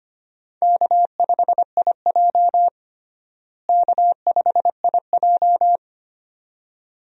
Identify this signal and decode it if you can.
Morse code, heard as a keyed tone, about 25 wpm, 700 Hz. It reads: K5IJ K5IJ
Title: USA Call Signs - 4 Characters - Repeated 2x - 25wpm